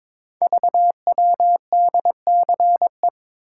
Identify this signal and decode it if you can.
Morse code, heard as a keyed tone, about 22 wpm, 700 Hz. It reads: VWDCE
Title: Code Groups - 5 Letters - 22wpm